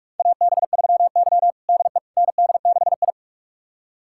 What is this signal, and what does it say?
Morse code, heard as a keyed tone, about 46 wpm, 700 Hz. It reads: AB3Y DE ND6I
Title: USA Callsign DE Callsign - 4 Characters - 45wpm